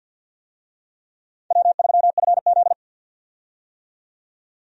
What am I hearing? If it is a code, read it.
Morse code, heard as a keyed tone, about 52 wpm, 700 Hz. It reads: W3FZ
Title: USA Call Signs - 4 Characters - 50wpm